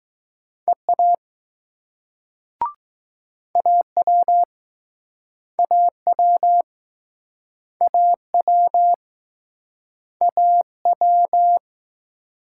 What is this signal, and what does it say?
Morse code, heard as a keyed tone, about 20 wpm, 700 Hz. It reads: EA AW AW AW AW